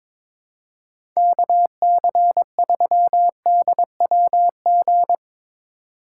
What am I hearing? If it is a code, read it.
Morse code, heard as a keyed tone, about 22 wpm, 700 Hz. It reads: KC3DWG
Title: USA Call Signs - 6 Characters - 2x Character Spacing - 22wpm